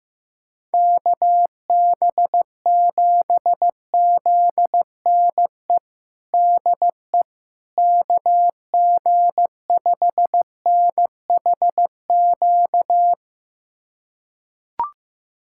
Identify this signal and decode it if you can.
Morse code, heard as a keyed tone, about 15 wpm, 700 Hz. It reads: KB7ZNE DE KG5NHQ
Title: USA Callsign DE Callsign 15wpm